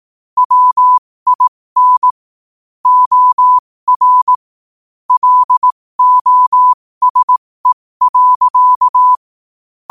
Morse code, heard as a keyed tone, about 18 wpm, spaced as for 15 wpm, 1000 Hz. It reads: WIN OR LOSE.
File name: qotdfiles / 15wpm